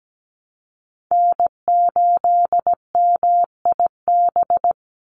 Morse code, heard as a keyed tone, about 17 wpm, 700 Hz. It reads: N8MIB
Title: USA Call Signs 17wpm